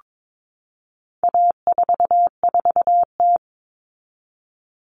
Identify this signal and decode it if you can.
Morse code, heard as a keyed tone, about 22 wpm, 700 Hz. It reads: A44T
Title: All Cut Numbers - 4-Digit 22wpm